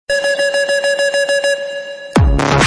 There is no code.